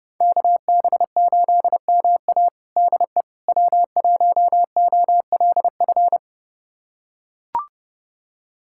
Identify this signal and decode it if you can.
Morse code, heard as a keyed tone, about 30 wpm, 700 Hz. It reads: KB8MA DE W1OLF
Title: USA Callsign DE Callsign - 5 Characters - 30wpm